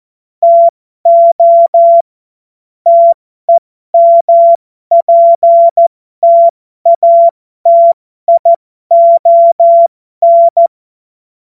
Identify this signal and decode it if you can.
Morse code, heard as a keyed tone, about 14 wpm, spaced as for 12 wpm, 680 Hz. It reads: TO TEMPTATION
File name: Morsekurs / Witze